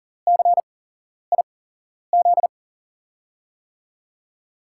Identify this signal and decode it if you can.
Morse code, heard as a keyed tone, about 42 wpm, spaced as for 11 wpm, 700 Hz. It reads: CIZ
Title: USA Call Signs - 6 Characters - 8x Character Spacing - 40wpm